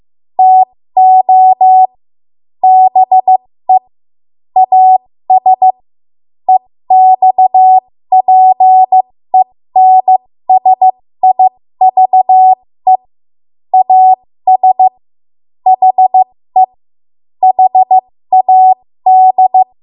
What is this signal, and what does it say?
Morse code, heard as a keyed tone, about 15 wpm, spaced as for 13 wpm, 750 Hz. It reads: TO BE AS EXPENSIVE AS HE HAD
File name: Morse / 13WPM